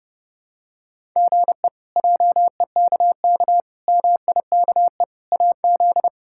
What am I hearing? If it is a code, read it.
Morse code, heard as a keyed tone, about 32 wpm, 700 Hz. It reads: GE JEKK MIKE AZ
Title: K1USN SST QSO Practice - 30wpm